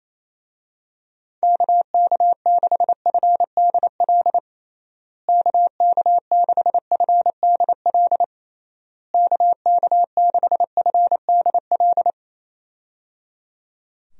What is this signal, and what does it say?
Morse code, heard as a keyed tone, about 28 wpm, 700 Hz. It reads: KK6FDL KK6FDL KK6FDL
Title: USA Call Signs - 6 Characters - Repeated 3x - 28wpm